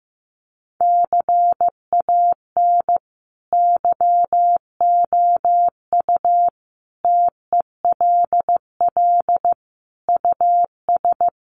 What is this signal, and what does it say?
Morse code, heard as a keyed tone, about 15 wpm, 700 Hz. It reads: CAN YOU TELL US
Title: Sets of 5 Words Encore2 15wpm